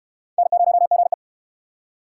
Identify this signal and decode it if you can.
Morse code, heard as a keyed tone, about 66 wpm, 700 Hz. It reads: N0GE